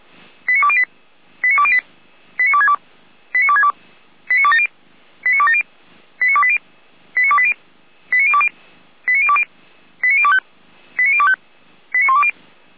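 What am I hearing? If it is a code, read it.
Five-tone selective call (ZVEI), heard as quick runs of five tones, tones of about 70 ms. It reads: 89208 89208 89262 89262 89270 89270 89280 89280 89020 89020 89025 89025 89120
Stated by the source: ZVEI five-tone selective call received by SDR.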